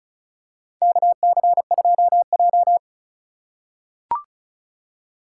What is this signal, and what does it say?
Morse code, heard as a keyed tone, about 34 wpm, 700 Hz. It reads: KC2J
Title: USA Call Signs - 3x Character Spacing - 35wpm